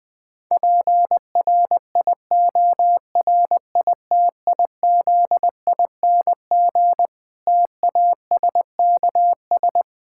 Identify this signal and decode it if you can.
Morse code, heard as a keyed tone, about 20 wpm, 700 Hz. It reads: PRIORITIZING TASKS
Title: Difficult Sentences with Hints - 20wpm